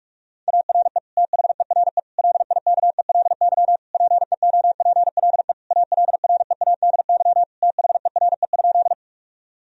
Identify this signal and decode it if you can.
Morse code, heard as a keyed tone, about 41 wpm, 700 Hz. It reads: ARE THERE LIKELY PEOPLE ALREADY THERE?
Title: Sentences from Top 1000 Words - Review Part 4 - 45wpm